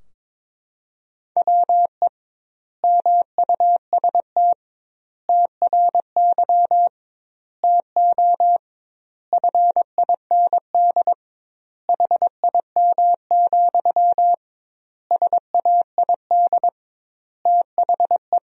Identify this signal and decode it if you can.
Morse code, heard as a keyed tone, about 22 wpm, 700 Hz. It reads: WE MUST TRY TO FIND HIM, SAID THE
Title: 2x Word-Spacing - Sentences from Top 400 Words - 22wpm